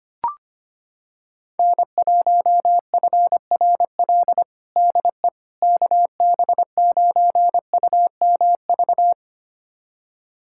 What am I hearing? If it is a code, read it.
Morse code, heard as a keyed tone, about 25 wpm, 700 Hz. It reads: N1FRL DE KB9UMV
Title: USA Callsign DE Callsign 25wpm